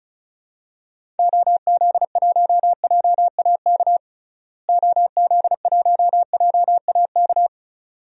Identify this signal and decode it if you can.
Morse code, heard as a keyed tone, about 36 wpm, 700 Hz. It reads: OZ1JAK OZ1JAK
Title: DXCC-100 Call Signs - 6 Characters - Repeated 2x - 35wpm